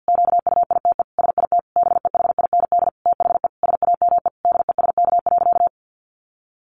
Morse code, heard as a keyed tone, about 50 wpm, 700 Hz. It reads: QUITE SIT BEHIND THE SAME DESK.
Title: A Sound Of Thunder 50wpm